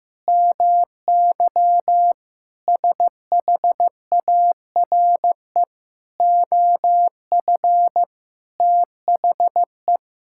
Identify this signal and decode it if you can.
Morse code, heard as a keyed tone, about 16 wpm, 700 Hz. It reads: MY SHARE OF THE
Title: Sentences from Top 1000 Words - Review Part 6 - 15wpm